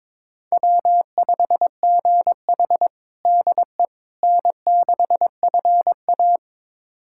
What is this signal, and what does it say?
Morse code, heard as a keyed tone, about 22 wpm, 700 Hz. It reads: W5GH DE N6FA
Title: USA Callsign DE Callsign - 4 Characters - 22wpm